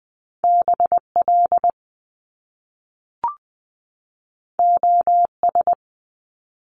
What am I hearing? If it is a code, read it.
Morse code, heard as a keyed tone, about 20 wpm, 700 Hz. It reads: BL OS
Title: Top Two Letter Combinations in Words 20wpm